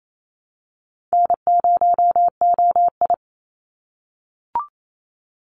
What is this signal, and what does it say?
Morse code, heard as a keyed tone, about 28 wpm, 700 Hz. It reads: N0OI
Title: NA SOTA Activator Callsigns 28wpm